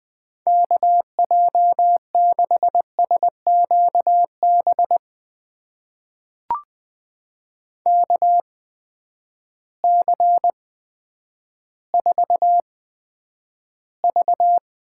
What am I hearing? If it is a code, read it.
Morse code, heard as a keyed tone, about 20 wpm, spaced as for 10 wpm, 700 Hz. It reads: KJ6SQB KC4V
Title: USA Call Signs - 6 Characters - 8x Character Spacing - 20wpm